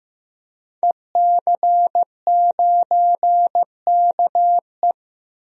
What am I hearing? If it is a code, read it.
Morse code, heard as a keyed tone, about 15 wpm, 700 Hz. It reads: EC9KE
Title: DXCC-100 Call Signs - 5 Characters - 15wpm